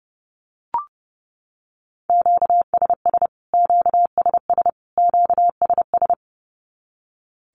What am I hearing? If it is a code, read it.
Morse code, heard as a keyed tone, about 30 wpm, 700 Hz. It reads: QSS QSS QSS
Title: Q-Codes Repeated 3x 30wpm